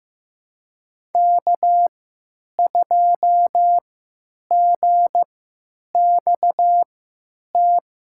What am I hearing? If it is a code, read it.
Morse code, heard as a keyed tone, about 15 wpm, spaced as for 9 wpm, 700 Hz. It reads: K2GXT
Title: USA Call Signs - 5 Characters - 3x Character Spacing - 15wpm